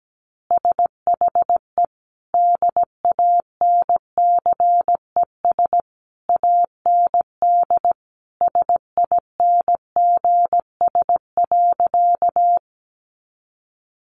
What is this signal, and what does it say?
Morse code, heard as a keyed tone, about 17 wpm, 700 Hz. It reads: SHE DANCES AND SINGS.
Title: Easy Phrases 17wpm